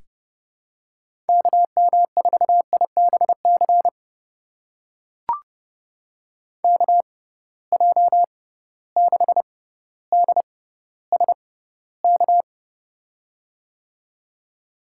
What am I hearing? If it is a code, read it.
Morse code, heard as a keyed tone, about 30 wpm, spaced as for 15 wpm, 700 Hz. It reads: KM4IBC KJ6DSK
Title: USA Call Signs - 6 Characters - 6x Character Spacing - 30wpm